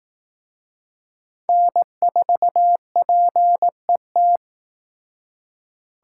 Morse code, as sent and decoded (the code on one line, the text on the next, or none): -. ....- .--. -
N4PET